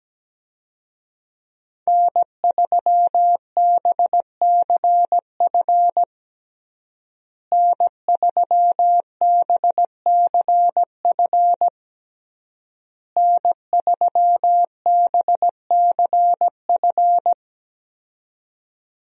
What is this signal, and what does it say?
Morse code, heard as a keyed tone, about 17 wpm, 700 Hz. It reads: N3BCF N3BCF N3BCF
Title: USA Call Signs - 5 Characters - Repeated 3x - 17wpm